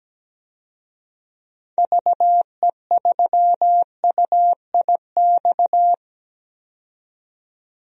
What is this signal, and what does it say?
Morse code, heard as a keyed tone, about 17 wpm, 700 Hz. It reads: VE3UIX
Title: DXCC-100 Call Signs - 6 Characters - 17wpm